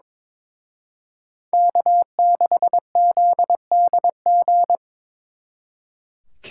Morse code, heard as a keyed tone, about 26 wpm, 700 Hz. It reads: K6ZDG
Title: Speed Racing - USA 1x3 Call Signs 22wpm